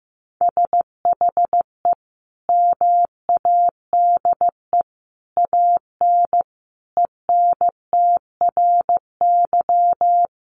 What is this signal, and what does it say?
Morse code, heard as a keyed tone, about 15 wpm, 700 Hz. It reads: SHE MADE AN ENTRY
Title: Sentences from Top 1500 Words 15wpm